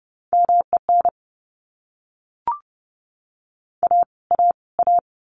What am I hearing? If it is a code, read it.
Morse code, heard as a keyed tone, about 30 wpm, 700 Hz. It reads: MEN A A A
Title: Top 500 Words Repeated 30wpm